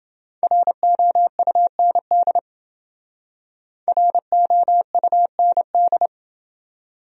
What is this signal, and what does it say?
Morse code, heard as a keyed tone, about 28 wpm, 700 Hz. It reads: ROUND ROUND